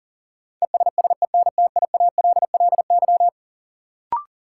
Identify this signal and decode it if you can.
Morse code, heard as a keyed tone, about 40 wpm, 700 Hz. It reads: ESSENTIALLY
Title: Sentences from Top 1300 Words - New Word Drill Alt - 40wpm